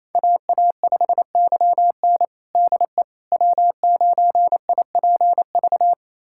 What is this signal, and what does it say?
Morse code, heard as a keyed tone, about 28 wpm, 700 Hz. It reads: AA5YN DE W9IPV
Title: USA Callsign DE Callsign - 5 Characters - 28wpm